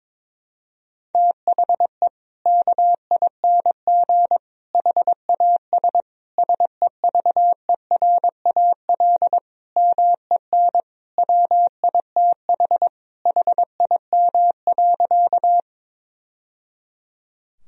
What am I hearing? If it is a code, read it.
Morse code, heard as a keyed tone, about 21 wpm, 700 Hz. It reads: THE KING HAS SEVERAL MEN WITH HIM.
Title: Sentences from Top 1000 Words - Review Part 2 - 22wpm